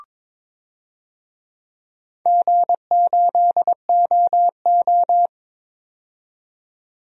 Morse code, heard as a keyed tone, about 22 wpm, 700 Hz. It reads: G8OO
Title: DXCC-100 Call Signs - 4 Characters - 22wpm